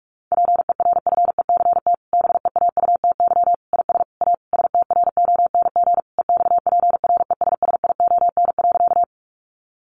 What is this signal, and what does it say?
Morse code, heard as a keyed tone, about 44 wpm, 700 Hz. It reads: PERFECT BEAUTY IS A STRONG EXPRESSION.
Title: Sentences from Top 1200 Words 45wpm